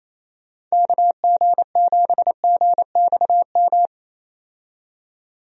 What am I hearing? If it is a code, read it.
Morse code, heard as a keyed tone, about 28 wpm, 700 Hz. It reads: KG7GXM
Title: USA Call Signs - 6 Characters - 28wpm